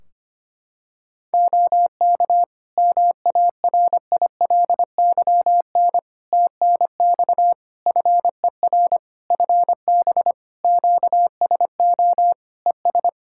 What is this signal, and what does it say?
Morse code, heard as a keyed tone, about 25 wpm, 700 Hz. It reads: OK MARILYN TNX FER FB QSO ES
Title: Practice QSO - Exchange 4 - Ending - 25wpm